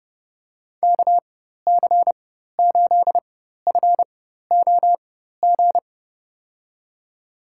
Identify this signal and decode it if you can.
Morse code, heard as a keyed tone, about 30 wpm, spaced as for 14 wpm, 700 Hz. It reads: KC8FOG
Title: USA Call Signs - 6 Characters - 4x Character Spacing - 30wpm